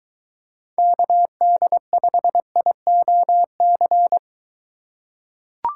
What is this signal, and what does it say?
Morse code, heard as a keyed tone, about 22 wpm, 700 Hz. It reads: KD5IOC